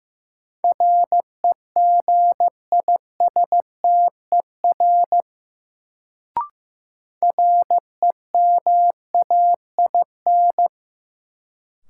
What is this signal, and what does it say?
Morse code, heard as a keyed tone, about 15 wpm, 700 Hz. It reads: REGISTER REMAIN
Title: Sentences from Top 1200 Words - New Word Drill Alt - 15wpm